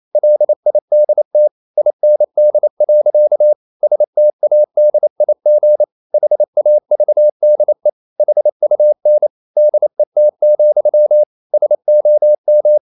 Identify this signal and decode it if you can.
Morse code, heard as a keyed tone, about 28 wpm, 600 Hz. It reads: LIDT IND. STADIG HAVDE HUN DET, SOM
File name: Ferie QRQ 28wpm